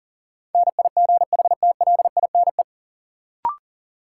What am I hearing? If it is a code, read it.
Morse code, heard as a keyed tone, about 40 wpm, 700 Hz. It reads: NIGHTLINE
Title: Sets of 5 Words - New Word Drill - 40wpm